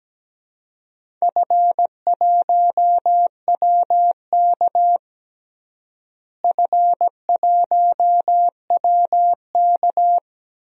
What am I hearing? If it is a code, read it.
Morse code, heard as a keyed tone, about 17 wpm, 700 Hz. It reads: F1WK F1WK
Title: DXCC-100 Call Signs - 4 Characters - Repeated 3x - 17wpm